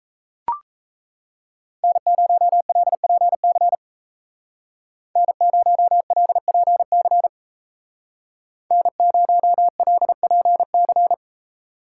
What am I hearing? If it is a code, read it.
Morse code, heard as a keyed tone, about 34 wpm, 700 Hz. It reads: N0LPC N0LPC N0LPC
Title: Speed Racing - USA 1x3 Call Signs 28wpm